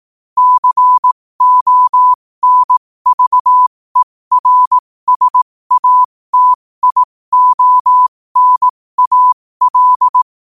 Morse code, heard as a keyed tone, about 18 wpm, spaced as for 15 wpm, 1000 Hz. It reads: CONVERSATIONAL
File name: qotdfiles / 15wpm